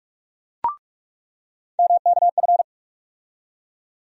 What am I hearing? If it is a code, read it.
Morse code, heard as a keyed tone, about 46 wpm, 700 Hz. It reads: MKF
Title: USA Call Sign Postfixes - 3 Letters 45wpm